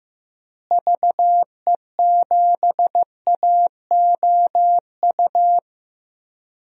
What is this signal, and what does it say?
Morse code, heard as a keyed tone, about 15 wpm, 700 Hz. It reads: VE7AOU
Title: DXCC-100 Call Signs - 6 Characters - Repeated 2x - 15wpm